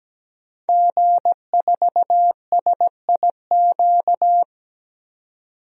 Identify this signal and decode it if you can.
Morse code, heard as a keyed tone, about 17 wpm, 700 Hz. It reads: G4SIQ
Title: DXCC-100 Call Signs - 5 Characters - 17wpm